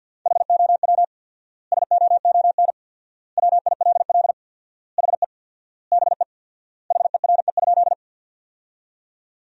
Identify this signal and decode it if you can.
Morse code, heard as a keyed tone, about 50 wpm, 700 Hz. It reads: HOW SOON WILL HE BE HERE?
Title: 4x Word-Spacing - Sentences from Top 400 Words - 50wpm